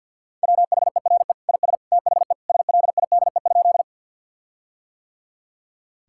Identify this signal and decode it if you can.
Morse code, heard as a keyed tone, about 50 wpm, 700 Hz. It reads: WHERE IS THE SLIDE?
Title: Sentences from Top 1600 Words 50wpm